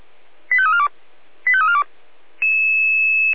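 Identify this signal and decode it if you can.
Five-tone selective call (ZVEI), heard as quick runs of five tones, tones of about 70 ms. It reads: 85342 85342
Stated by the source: ZVEI five-tone selective call received on SDR.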